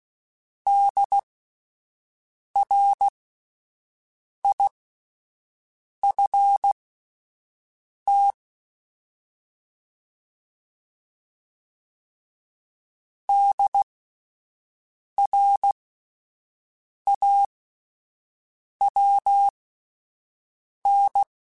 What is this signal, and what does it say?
Morse code, heard as a keyed tone, about 16 wpm, spaced as for 5 wpm, 800 Hz. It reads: DRIFT DRAWN